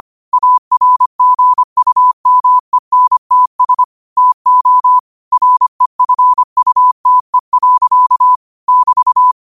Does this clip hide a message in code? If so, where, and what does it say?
Yes: Morse code from 0.3 to 9.3 s, heard as a keyed tone, about 25 wpm, 1000 Hz. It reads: ARGUMENTS TO REFUTE. <BT>